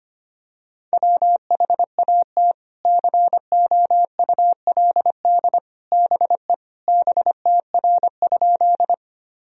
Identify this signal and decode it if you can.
Morse code, heard as a keyed tone, about 25 wpm, 700 Hz. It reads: WHAT COULD BE BTR?
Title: Sentences from Top 400 Words and Abbreviations 25wpm